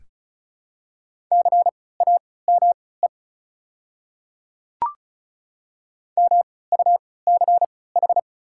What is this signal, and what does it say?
Morse code, heard as a keyed tone, about 34 wpm, spaced as for 20 wpm, 700 Hz. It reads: CAME MUCH
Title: 3x Character Spacing - Top 200 Words - 35wpm